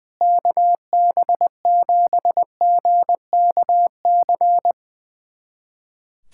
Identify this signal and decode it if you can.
Morse code, heard as a keyed tone, about 21 wpm, 700 Hz. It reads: KB7GKC
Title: USA Callsign DE Callsign - 6 Characters - 20wpm